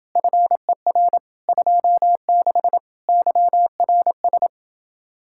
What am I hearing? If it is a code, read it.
Morse code, heard as a keyed tone, about 30 wpm, 700 Hz. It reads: FER 26 YRS